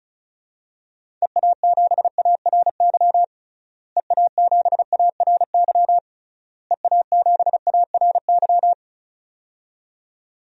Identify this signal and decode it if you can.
Morse code, heard as a keyed tone, about 36 wpm, 700 Hz. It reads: EA7ARY EA7ARY EA7ARY
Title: DXCC-100 Call Signs - 6 Characters - Repeated 3x - 35wpm